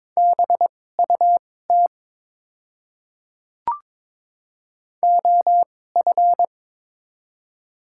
Morse code, heard as a keyed tone, about 22 wpm, spaced as for 16 wpm, 700 Hz. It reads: BUT OF
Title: 2x Character Spacing - Top 100 Words - 22wpm